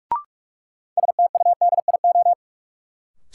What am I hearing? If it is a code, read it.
Morse code, heard as a keyed tone, about 46 wpm, 700 Hz. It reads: STUDIO
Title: Sentences from Top 1200 Words - New Word Drill - 45wpm